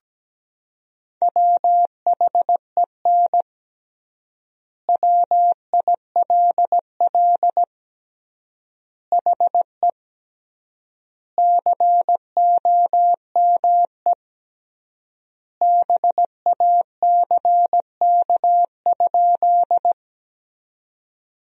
Morse code, heard as a keyed tone, about 17 wpm, 700 Hz. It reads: WHEN WILL HE COME BACK?
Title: 3x Word-Spacing - Sentences from Top 200 Words - 17wpm